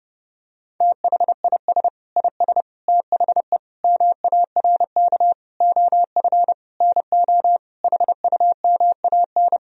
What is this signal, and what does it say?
Morse code, heard as a keyed tone, about 30 wpm, 700 Hz. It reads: THIS IS THE MARK OF NO HUMAN
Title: Sentences from Top 1000 Words - Review Part 3 - 30wpm